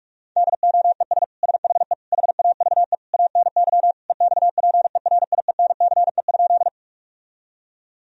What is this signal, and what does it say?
Morse code, heard as a keyed tone, about 46 wpm, 700 Hz. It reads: DOES SHE HAVE ANY EXPERIENCE?
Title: Sentences from Top 1000 Words - Review Part 3 - 45wpm